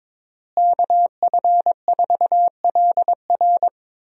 Morse code, heard as a keyed tone, about 22 wpm, 700 Hz. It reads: KF4LR
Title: USA Call Signs - 5 Characters - Repeated 3x - 22wpm